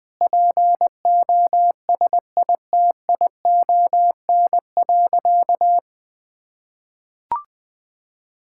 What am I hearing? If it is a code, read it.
Morse code, heard as a keyed tone, about 20 wpm, 700 Hz. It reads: POSITION.
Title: Sentences from Top 1000 Words - Review Part 5 - 20wpm